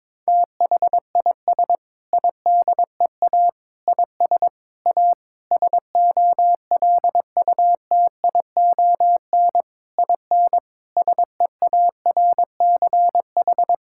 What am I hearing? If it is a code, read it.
Morse code, heard as a keyed tone, about 22 wpm, 700 Hz. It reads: THIS IDEA IS A SOLUTION IN SEARCH
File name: practice / 22wpm / Sentences with 8-Word Phrases - 22wpm